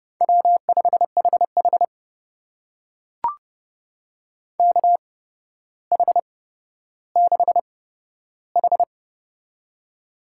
Morse code, heard as a keyed tone, about 30 wpm, spaced as for 13 wpm, 700 Hz. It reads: W5HH KH6H
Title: USA Call Signs - 4 Characters - 8x Character Spacing - 30wpm